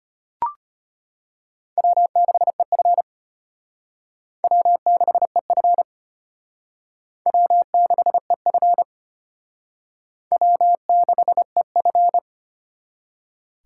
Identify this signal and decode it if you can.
Morse code, heard as a keyed tone, about 31 wpm, 700 Hz. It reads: W6EF W6EF W6EF W6EF
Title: Speed Racing - USA 1x2 Call Signs 25wpm